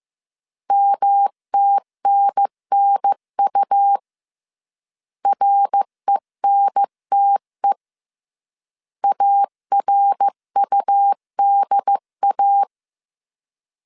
Morse code, heard as a keyed tone, about 15 wpm, 800 Hz. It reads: MTNNU RENTE ARUDA
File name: CW / DET / 1 3 1500C S C 14wpm 2458